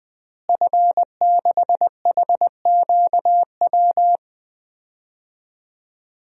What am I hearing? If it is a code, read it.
Morse code, heard as a keyed tone, about 20 wpm, 700 Hz. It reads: F6HQW
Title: DXCC-100 Call Signs - 5 Characters - Repeated 2x - 20wpm